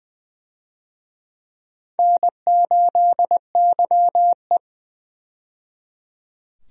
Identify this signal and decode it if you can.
Morse code, heard as a keyed tone, about 20 wpm, 700 Hz. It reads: N8YE
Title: USA Call Signs - 4 Characters - 20wpm